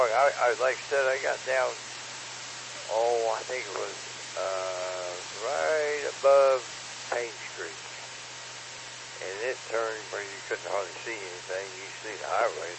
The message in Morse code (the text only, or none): none